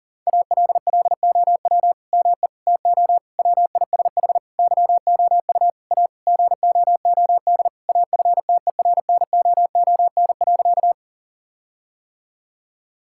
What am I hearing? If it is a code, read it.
Morse code, heard as a keyed tone, about 40 wpm, 700 Hz. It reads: ALLOW ME TO WISH YOU A GOOD AFTERNOON.
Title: Sentences from Top 1600 Words 40wpm